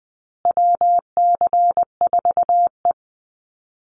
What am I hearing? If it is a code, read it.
Morse code, heard as a keyed tone, about 20 wpm, 700 Hz. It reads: WC4E
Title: Call Signs Repeated 2x - 20wpm